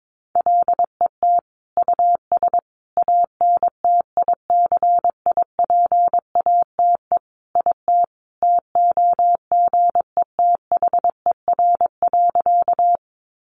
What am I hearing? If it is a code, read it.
Morse code, heard as a keyed tone, about 22 wpm, 700 Hz. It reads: LET US ANTICIPATE IT TOGETHER.
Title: Sentences from Top 1500 Words 22wpm